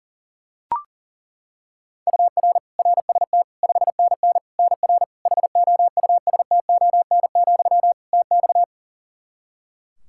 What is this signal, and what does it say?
Morse code, heard as a keyed tone, about 40 wpm, 700 Hz. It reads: UR RST 5NN NR HOUSTON, TX
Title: K1USN SST QSO Practice 40wpm